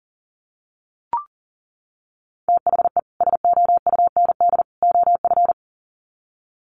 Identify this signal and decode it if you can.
Morse code, heard as a keyed tone, about 40 wpm, 700 Hz. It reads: THE SOUND OF
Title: Sets of 3 Words 40wpm